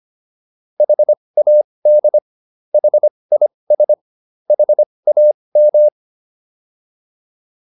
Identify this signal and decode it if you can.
Morse code, heard as a keyed tone, about 25 wpm, spaced as for 20 wpm, 600 Hz. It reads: HAD HIS HAM